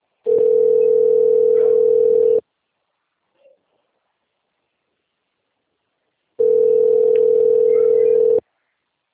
Ringback tone (North America), 440 + 480 Hz.
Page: Aracatuba - SP